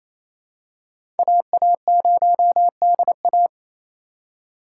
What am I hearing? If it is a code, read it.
Morse code, heard as a keyed tone, about 28 wpm, 700 Hz. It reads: AA0DA
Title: USA Call Signs - 4x Character Spacing - 28wpm